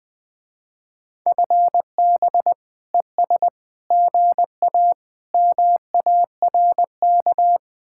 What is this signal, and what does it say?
Morse code, heard as a keyed tone, about 20 wpm, 700 Hz. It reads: FB ES GA MARK
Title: Speed-Racing Common QSO Phrases - 17wpm